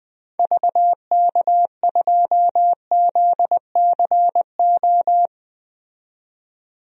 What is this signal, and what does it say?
Morse code, heard as a keyed tone, about 20 wpm, 700 Hz. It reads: VK2ZCO
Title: DXCC-100 Call Signs - 6 Characters - 20wpm